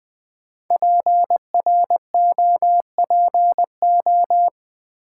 Morse code, heard as a keyed tone, about 20 wpm, 700 Hz. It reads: PROPO
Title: Build A Word - 5 Letters At A Time - 10-Letter Words - 20wpm